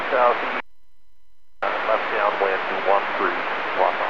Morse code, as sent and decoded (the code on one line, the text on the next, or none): none